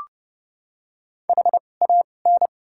Morse code, heard as a keyed tone, about 30 wpm, spaced as for 22 wpm, 700 Hz. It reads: HAN
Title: 2x Character Spacing - Top 500 Words - 30wpm